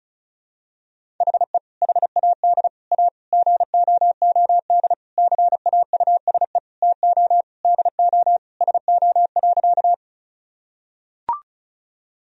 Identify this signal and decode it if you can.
Morse code, heard as a keyed tone, about 36 wpm, 700 Hz. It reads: HE HAD A GOOD CAUSE TO DO SO.
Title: Sentences from Top 1000 Words - Review Part 1 - 35wpm